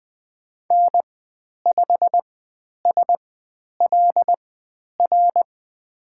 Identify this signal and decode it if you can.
Morse code, heard as a keyed tone, about 20 wpm, spaced as for 10 wpm, 700 Hz. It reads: N5SLR
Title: 095 Lesson 16g - Letter c - Call Signs 20wpm